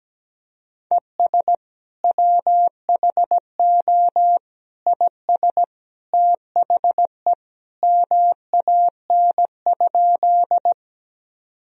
Morse code, heard as a keyed tone, about 17 wpm, 700 Hz. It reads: ES WHO IS THE MAN?
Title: Sentences from Top 200 Words and Abbreviations 17wpm